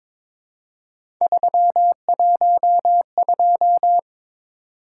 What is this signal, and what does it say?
Morse code, heard as a keyed tone, about 23 wpm, 700 Hz. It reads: 312
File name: practice / 22wpm / Three Letter-Number Combinations - 22wpm